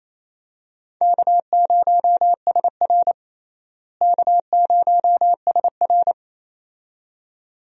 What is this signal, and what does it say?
Morse code, heard as a keyed tone, about 28 wpm, 700 Hz. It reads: K0SR K0SR
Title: USA Call Signs - 4 Characters - Repeated 2x - 28wpm